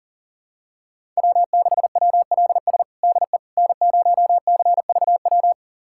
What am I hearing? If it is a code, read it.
Morse code, heard as a keyed tone, about 40 wpm, 700 Hz. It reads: W6WLS DE N0CVW
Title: USA Callsign DE Callsign - 5 Characters - 40wpm